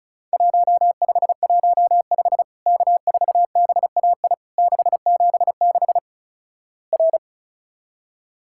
Morse code, heard as a keyed tone, about 36 wpm, 700 Hz. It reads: 1515 K4BAI 676 R
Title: Worked All Europe DX Contest - QTC Exchange - 35wpm